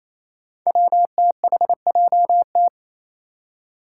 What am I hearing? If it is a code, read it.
Morse code, heard as a keyed tone, about 28 wpm, 700 Hz. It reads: WTHJT
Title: Five Challenging Letter-Number Combinations 28wpm